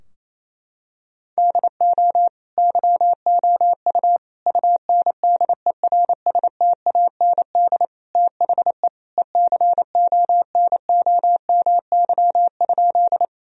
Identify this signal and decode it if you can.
Morse code, heard as a keyed tone, about 28 wpm, 700 Hz. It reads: DO YOU UNDERSTAND THE ECONOMY?
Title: Sentences from Top 1000 Words - Review Part 3 - 28wpm